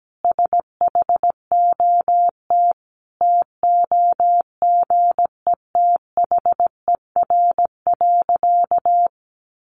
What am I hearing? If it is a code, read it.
Morse code, heard as a keyed tone, about 17 wpm, 700 Hz. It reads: SHOT TOGETHER.
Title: Sentences from Top 1100 Words 17wpm